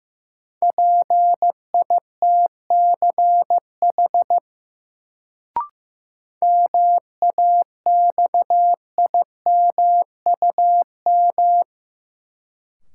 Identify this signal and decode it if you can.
Morse code, heard as a keyed tone, about 15 wpm, 700 Hz. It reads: PITCH MAXIMUM
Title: Sentences from Top 1300 Words - New Word Drill Alt - 15wpm